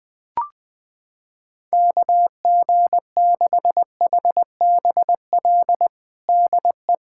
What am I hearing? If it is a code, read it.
Morse code, heard as a keyed tone, about 20 wpm, 700 Hz. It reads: KG6HBL DE
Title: K1USN SST QSO Practice 20wpm